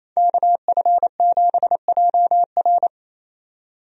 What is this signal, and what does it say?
Morse code, heard as a keyed tone, about 28 wpm, 700 Hz. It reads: KF7JR
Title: USA Call Signs - 5 Characters - 28wpm